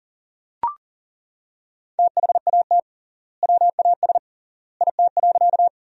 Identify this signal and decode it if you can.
Morse code, heard as a keyed tone, about 40 wpm, 700 Hz. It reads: THAT WAS IT.
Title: 3x Word-Spacing - Sentences from Top 100 Words - 40wpm